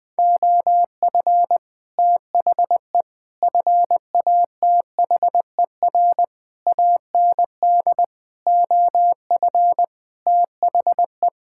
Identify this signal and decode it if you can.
Morse code, heard as a keyed tone, about 20 wpm, 700 Hz. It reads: OF THE FATHER AND OF THE
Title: Sentences with 9-Word Phrases - 20wpm